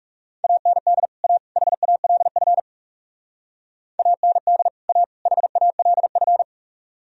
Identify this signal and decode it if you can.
Morse code, heard as a keyed tone, about 38 wpm, 700 Hz. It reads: AND A HALF AND A HALF